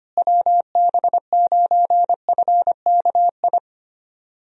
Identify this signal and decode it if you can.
Morse code, heard as a keyed tone, about 25 wpm, 700 Hz. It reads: WB9FKI
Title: USA Call Signs - 6 Characters - Repeated 3x - 25wpm